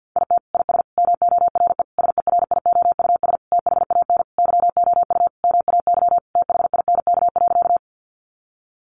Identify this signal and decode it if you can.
Morse code, heard as a keyed tone, about 50 wpm, 700 Hz. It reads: IT IS MORE SERIOUS THAN YOU MAY THINK.
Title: Sentences from Top 1100 Words 50wpm